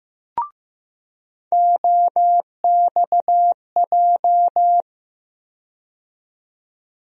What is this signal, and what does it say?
Morse code, heard as a keyed tone, about 15 wpm, 700 Hz. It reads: OXJ
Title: USA Call Sign Postfixes - 3 Letters 15wpm